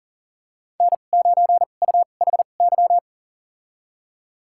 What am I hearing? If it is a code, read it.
Morse code, heard as a keyed tone, about 40 wpm, spaced as for 29 wpm, 700 Hz. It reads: N9UHY
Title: USA Call Signs - 5 Characters - 2x Character Spacing - 40wpm